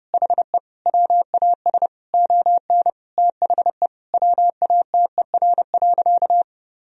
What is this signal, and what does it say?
Morse code, heard as a keyed tone, about 30 wpm, 700 Hz. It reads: HE WAS ON THE WATER.
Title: Sentences from Top 500 Words and Abbreviations 30wpm